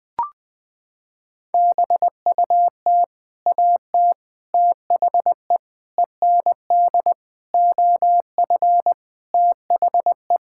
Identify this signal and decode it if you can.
Morse code, heard as a keyed tone, about 20 wpm, 700 Hz. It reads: BUT AT THE END OF THE
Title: Sentences with 7-Word Phrases - 20wpm